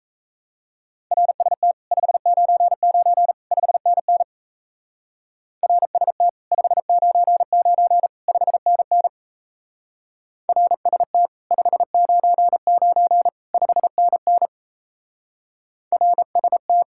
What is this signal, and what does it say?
Morse code, heard as a keyed tone, about 36 wpm, 700 Hz. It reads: RST 599 5NN RST 599 5NN RST 599 5NN RST